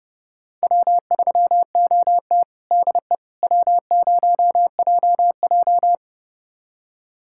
Morse code, heard as a keyed tone, about 30 wpm, 700 Hz. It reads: W3OT DE W0JJ
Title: USA Callsign DE Callsign - 4 Characters - 30wpm